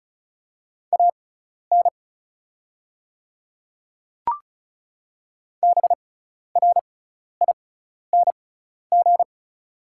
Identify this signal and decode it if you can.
Morse code, heard as a keyed tone, about 36 wpm, spaced as for 12 wpm, 700 Hz. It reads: AN BRING
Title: 6x Character Spacing - Top 500 Words - 35wpm